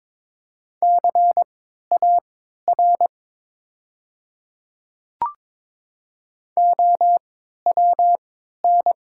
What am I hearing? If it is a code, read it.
Morse code, heard as a keyed tone, about 21 wpm, spaced as for 12 wpm, 690 Hz. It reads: CAR OWN
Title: 3x Character Spacing - Top 500 Words - 22wpm